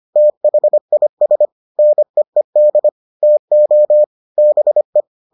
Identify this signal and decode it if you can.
Morse code, heard as a keyed tone, about 25 wpm, 600 Hz. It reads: THIS NEED TO BE